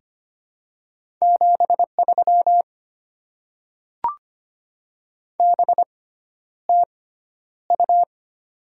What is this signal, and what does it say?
Morse code, heard as a keyed tone, about 25 wpm, spaced as for 11 wpm, 700 Hz. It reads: 73 BTU
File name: Top 20 QSO Elements - 6x Character Spacing - 25wpm